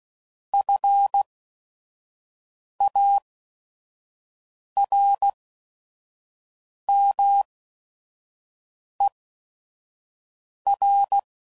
Morse code, heard as a keyed tone, about 16 wpm, spaced as for 5 wpm, 800 Hz. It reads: FARMER
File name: CW Code Groups @5WPM / fromSM's / Cities